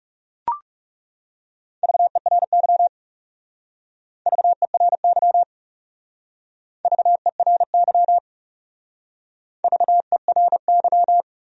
Morse code, heard as a keyed tone, about 38 wpm, 700 Hz. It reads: VERY VERY VERY VERY